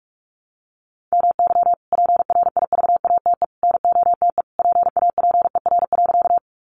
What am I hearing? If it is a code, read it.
Morse code, heard as a keyed tone, about 44 wpm, 700 Hz. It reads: MY PRIVATE NOTE PAPER.
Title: Sentences from Top 1000 Words 45wpm